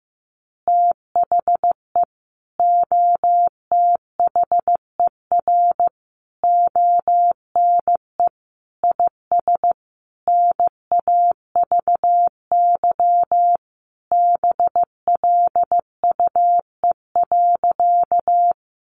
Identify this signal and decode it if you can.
Morse code, heard as a keyed tone, about 14 wpm, 700 Hz. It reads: THE OTHER ONE IS NAVY BLUE.
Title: Additional Easy Phrases 15wpm